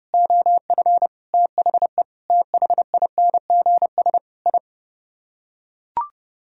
Morse code, heard as a keyed tone, about 30 wpm, 700 Hz. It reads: OF THE THINGS I